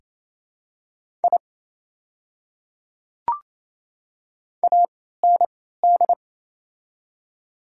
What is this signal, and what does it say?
Morse code, heard as a keyed tone, about 27 wpm, spaced as for 16 wpm, 700 Hz. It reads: I AND
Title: 3x Character Spacing - Top 300 Words - 28wpm